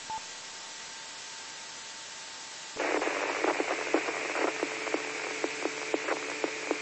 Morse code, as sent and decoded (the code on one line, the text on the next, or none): none